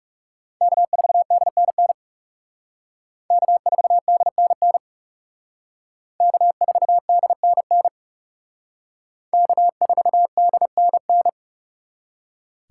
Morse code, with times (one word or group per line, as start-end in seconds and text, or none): 0.6-1.9 K4DNN
3.3-4.8 K4DNN
6.2-7.9 K4DNN
9.3-11.3 K4DNN